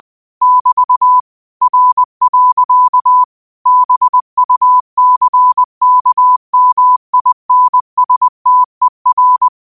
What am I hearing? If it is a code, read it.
Morse code, heard as a keyed tone, about 20 wpm, 1000 Hz. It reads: <BT> R. BUCKMINSTER